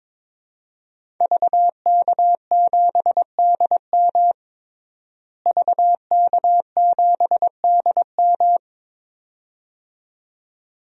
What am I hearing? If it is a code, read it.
Morse code, heard as a keyed tone, about 22 wpm, 700 Hz. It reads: VK7DM VK7DM
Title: DXCC-100 Call Signs - 5 Characters - Repeated 2x - 22wpm